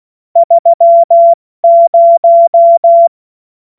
Morse code, heard as a keyed tone, about 16 wpm, 680 Hz. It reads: 30
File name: Morsekurs / Witze / Witz 19. April 2022 70 F80